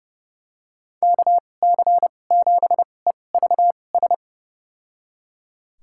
Morse code, heard as a keyed tone, about 30 wpm, spaced as for 22 wpm, 700 Hz. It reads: KC7EVS
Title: USA Call Signs - 6 Characters - 2x Character Spacing - 30wpm